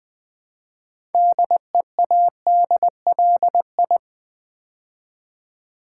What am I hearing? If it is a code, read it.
Morse code, heard as a keyed tone, about 20 wpm, 700 Hz. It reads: DEADLI
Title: Build A Word - 3 Letters At A Time - 8-Letter Words - 20wpm